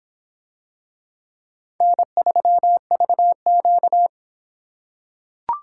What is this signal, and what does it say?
Morse code, heard as a keyed tone, about 26 wpm, 700 Hz. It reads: N3VQ